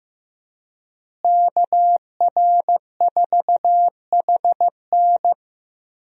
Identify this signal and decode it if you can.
Morse code, heard as a keyed tone, about 15 wpm, 700 Hz. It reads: KR4HN
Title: USA Call Signs - 5 Characters - Repeated 3x - 15wpm